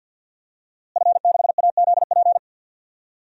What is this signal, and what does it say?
Morse code, heard as a keyed tone, about 50 wpm, 700 Hz. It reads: F6AZP
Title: DXCC-100 Call Signs - 5 Characters - Repeated 2x - 50wpm